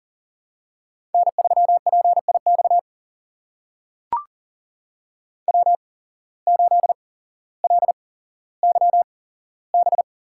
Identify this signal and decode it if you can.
Morse code, heard as a keyed tone, about 39 wpm, spaced as for 17 wpm, 700 Hz. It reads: N3PIX W8LYB
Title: USA Call Signs - 5 Characters - 8x Character Spacing - 40wpm